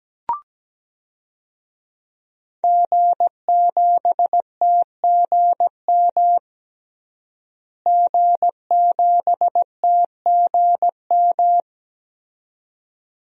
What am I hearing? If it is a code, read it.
Morse code, heard as a keyed tone, about 17 wpm, 700 Hz. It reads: G7TGM G7TGM
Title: DXCC-100 Call Signs - 5 Characters - Repeated 2x - 17wpm